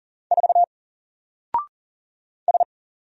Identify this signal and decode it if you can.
Morse code, heard as a keyed tone, about 39 wpm, 700 Hz. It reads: 4 S
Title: Single Character (Letter:Number:Punctuation:Prosign) Alt 40wpm